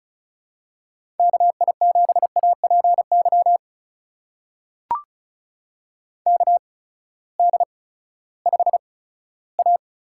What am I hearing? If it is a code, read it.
Morse code, heard as a keyed tone, about 34 wpm, spaced as for 17 wpm, 700 Hz. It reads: KI7APY KD5A
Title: USA Call Signs - 6 Characters - 8x Character Spacing - 35wpm